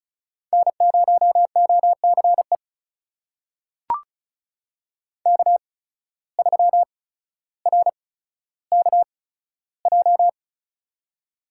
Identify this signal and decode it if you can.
Morse code, heard as a keyed tone, about 36 wpm, spaced as for 15 wpm, 700 Hz. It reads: N0OCE K3RKJ
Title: USA Call Signs - 5 Characters - 8x Character Spacing - 35wpm